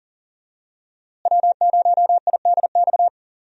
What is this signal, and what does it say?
Morse code, heard as a keyed tone, about 40 wpm, 690 Hz. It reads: W0IDX
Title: USA Call Signs - 5 Characters - Repeated 3x - 40wpm